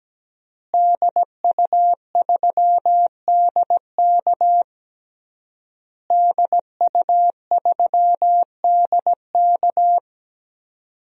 Morse code, heard as a keyed tone, about 17 wpm, 700 Hz. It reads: DU3DK DU3DK
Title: DXCC-100 Call Signs - 5 Characters - Repeated 3x - 17wpm